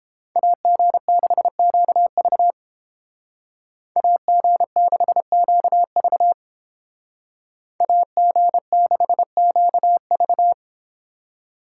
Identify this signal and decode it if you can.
Morse code, heard as a keyed tone, about 32 wpm, 700 Hz. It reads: AG6QV AG6QV AG6QV